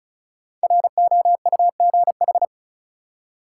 Morse code, heard as a keyed tone, about 36 wpm, 700 Hz. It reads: ROUGH
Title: Five Letter Words - 35wpm